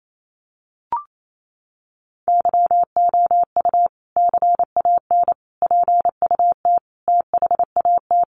Morse code, heard as a keyed tone, about 28 wpm, 700 Hz. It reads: YOU CAN PUT THAT
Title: Sentences from Top 1500 Words 28wpm